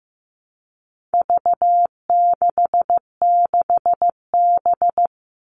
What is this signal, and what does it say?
Morse code, heard as a keyed tone, about 15 wpm, 700 Hz. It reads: V66B
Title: All Cut Numbers - 4-Digit 15wpm